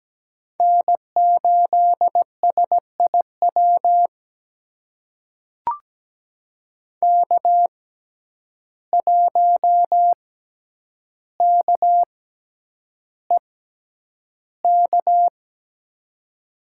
Morse code, heard as a keyed tone, about 17 wpm, spaced as for 9 wpm, 700 Hz. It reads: N8SIW K1KEK
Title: USA Call Signs - 5 Characters - 6x Character Spacing - 17wpm